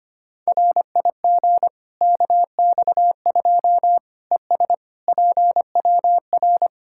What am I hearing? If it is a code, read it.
Morse code, heard as a keyed tone, about 25 wpm, 700 Hz. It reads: RIG KX2 ES PWR
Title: Practice QSO - Exchange 2 - Info - 25wpm